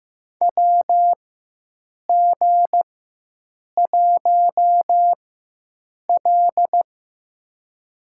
Morse code, heard as a keyed tone, about 15 wpm, spaced as for 7 wpm, 700 Hz. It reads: WG1L
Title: USA Call Signs - 4 Characters - 4x Character Spacing - 15wpm